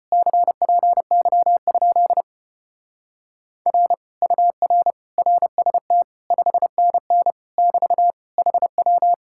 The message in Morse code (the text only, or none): CPY? R UR RST 5NN <BT> HW